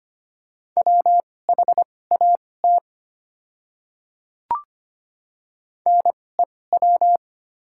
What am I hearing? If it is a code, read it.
Morse code, heard as a keyed tone, about 25 wpm, spaced as for 18 wpm, 700 Hz. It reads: WHAT NEW
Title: 2x Character Spacing - Top 300 Words - 25wpm